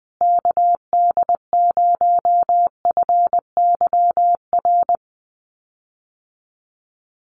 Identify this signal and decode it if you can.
Morse code, heard as a keyed tone, about 20 wpm, 700 Hz. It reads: KD0FYR
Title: USA Call Signs Repeated 2x 20wpm